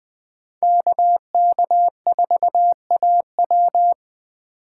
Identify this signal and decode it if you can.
Morse code, heard as a keyed tone, about 25 wpm, 700 Hz. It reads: KK4AW
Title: Speed Racing - USA Call Signs 20wpm